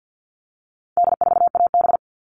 Morse code, heard as a keyed tone, about 50 wpm, 700 Hz. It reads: D4AB
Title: All Cut Numbers - 4-Digit 50wpm